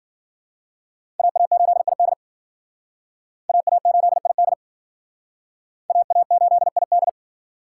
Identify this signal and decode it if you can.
Morse code, heard as a keyed tone, about 52 wpm, 700 Hz. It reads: AA8ID AA8ID AA8ID